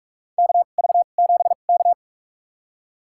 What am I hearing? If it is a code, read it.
Morse code, heard as a keyed tone, about 46 wpm, spaced as for 32 wpm, 700 Hz. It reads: KV7K
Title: USA Call Signs - 2x Character Spacing - 45wpm